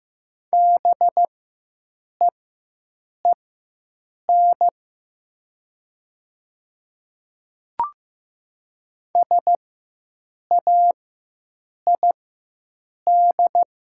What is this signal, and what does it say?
Morse code, heard as a keyed tone, about 15 wpm, spaced as for 7 wpm, 700 Hz. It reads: BEEN SAID
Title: 4x Character Spacing - Top 200 Words - 15wpm